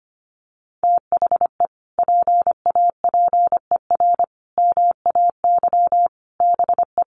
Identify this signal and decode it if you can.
Morse code, heard as a keyed tone, about 25 wpm, 700 Hz. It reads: THE PAPER MAY BE
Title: Sentences from Top 1400 Words 25wpm